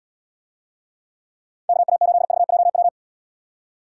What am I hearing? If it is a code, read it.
Morse code, heard as a keyed tone, about 73 wpm, 700 Hz. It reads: KA8VJW